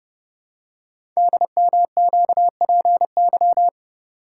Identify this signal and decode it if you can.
Morse code, heard as a keyed tone, about 29 wpm, 700 Hz. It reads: DMQPY
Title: Code Groups - 5 Letters - 30wpm